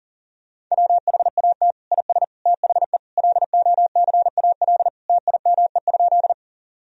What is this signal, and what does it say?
Morse code, heard as a keyed tone, about 40 wpm, 700 Hz. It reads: WHAT IS THE LOCAL TIME?
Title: Sentences from Top 1000 Words - Review Part 3 - 40wpm